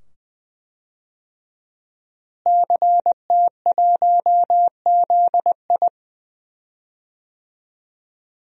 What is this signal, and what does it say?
Morse code, heard as a keyed tone, about 20 wpm, 700 Hz. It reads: CT1ZI
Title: DXCC-100 Call Signs - 5 Characters - Repeated 2x - 20wpm